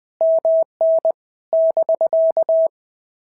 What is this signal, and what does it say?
Morse code, heard as a keyed tone, about 20 wpm, 650 Hz. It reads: MN <BK>
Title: POTA QSO Practice - 20wpm